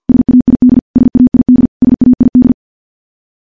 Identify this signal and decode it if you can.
Morse code, heard as a keyed tone, about 50 wpm, 250 Hz. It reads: SAID SAID SAID
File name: morse / morse-training-50WPM